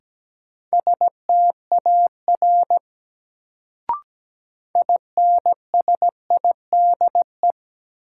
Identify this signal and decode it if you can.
Morse code, heard as a keyed tone, about 17 wpm, 700 Hz. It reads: STAR INSIDE
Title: Sentences from Top 1000 Words - Review Part 2 - New Word Drill - 17wpm